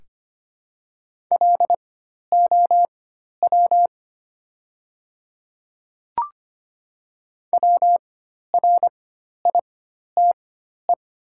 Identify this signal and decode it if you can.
Morse code, heard as a keyed tone, about 25 wpm, spaced as for 12 wpm, 700 Hz. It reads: LOW WRITE
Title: 4x Character Spacing - Top 200 Words - 25wpm